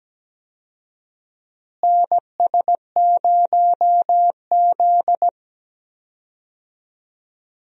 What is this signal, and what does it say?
Morse code, heard as a keyed tone, about 17 wpm, 700 Hz. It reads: NS0Z
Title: USA Call Signs - 4 Characters - Repeated 3x - 17wpm